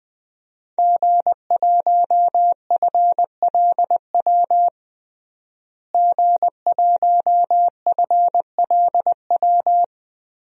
Morse code, heard as a keyed tone, about 21 wpm, 700 Hz. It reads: G1FLW G1FLW
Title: DXCC-100 Call Signs - 5 Characters - Repeated 3x - 20wpm